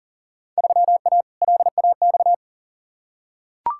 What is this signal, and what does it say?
Morse code, heard as a keyed tone, about 40 wpm, 700 Hz. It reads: 3A LAX
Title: Field Day Exchanges - 40wpm